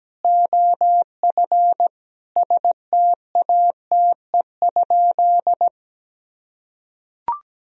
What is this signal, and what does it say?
Morse code, heard as a keyed tone, about 17 wpm, 700 Hz. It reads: OF STATE?
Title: Sentences from Top 1000 Words - Review Part 2 - 17wpm